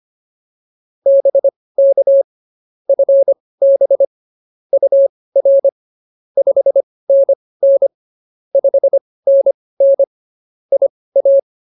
Morse code, heard as a keyed tone, about 24 wpm, spaced as for 18 wpm, 550 Hz. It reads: BK FB UR 5NN 5NN IA